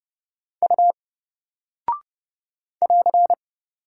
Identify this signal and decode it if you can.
Morse code, heard as a keyed tone, about 30 wpm, 700 Hz. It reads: U <AR>
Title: Single Character (Letter:Number:Punctuation:Prosign) 30wpm